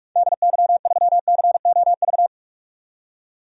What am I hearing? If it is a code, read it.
Morse code, heard as a keyed tone, about 44 wpm, 700 Hz. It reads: DY3COV
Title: DXCC-100 Call Signs - 6 Characters - 45wpm